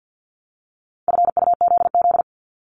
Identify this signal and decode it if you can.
Morse code, heard as a keyed tone, about 50 wpm, 700 Hz. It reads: FUZZ
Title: Multi-letter Phonemes 50wpm